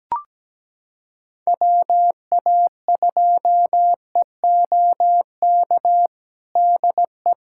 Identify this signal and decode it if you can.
Morse code, heard as a keyed tone, about 17 wpm, 700 Hz. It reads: WA2EOK DE
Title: USA Callsign DE Callsign - 6 Characters - 17wpm